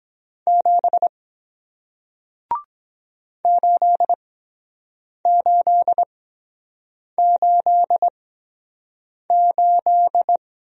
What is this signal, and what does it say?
Morse code, heard as a keyed tone, about 23 wpm, spaced as for 7 wpm, 700 Hz. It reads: 7 8888